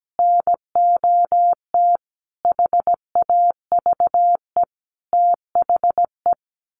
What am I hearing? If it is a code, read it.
Morse code, heard as a keyed tone, about 17 wpm, 700 Hz. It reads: NOT HAVE THE
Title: Sets of 5 Words 17wpm